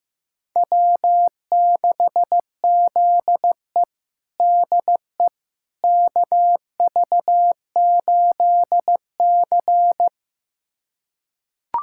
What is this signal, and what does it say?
Morse code, heard as a keyed tone, about 15 wpm, 700 Hz. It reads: W6ZE DE KV8C
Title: USA Callsign DE Callsign - 4 Characters - 15wpm